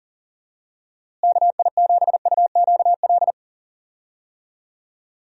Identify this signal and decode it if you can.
Morse code, heard as a keyed tone, about 40 wpm, 700 Hz. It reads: KI7UQL
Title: USA Call Signs - 6 Characters - Repeated 2x - 40wpm